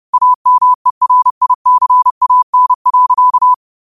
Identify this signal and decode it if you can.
Morse code, heard as a keyed tone, about 30 wpm, 1000 Hz. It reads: AMERICAN.
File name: qotdfiles / 30wpm